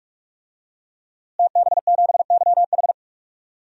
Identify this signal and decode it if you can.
Morse code, heard as a keyed tone, about 42 wpm, 700 Hz. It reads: TB7YH
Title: DXCC-100 Call Signs - 5 Characters - Repeated 3x - 45wpm